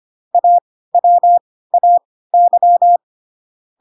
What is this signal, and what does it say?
Morse code, heard as a keyed tone, about 25 wpm, spaced as for 16 wpm, 700 Hz. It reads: AWAY
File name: Pinocchio / Pinocchio ch5 10 25 ik1hge